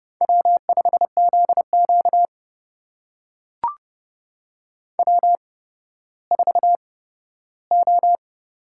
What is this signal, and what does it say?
Morse code, heard as a keyed tone, about 30 wpm, spaced as for 15 wpm, 700 Hz. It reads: W5ZQ W4O
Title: USA Call Signs - 4 Characters - 8x Character Spacing - 30wpm